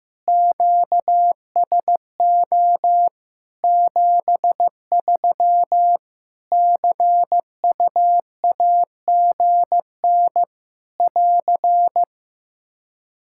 Morse code, heard as a keyed tone, about 14 wpm, 700 Hz. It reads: QSO 73 CUAGN <AR>
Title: Practice QSO - Exchange 4 - Ending - 15wpm